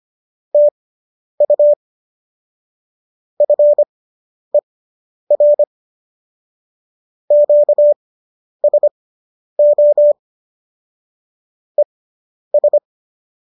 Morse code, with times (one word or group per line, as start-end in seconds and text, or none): 0.5-1.7 TU
3.4-5.6 FER
7.3-10.1 QSO
11.8-12.8 ES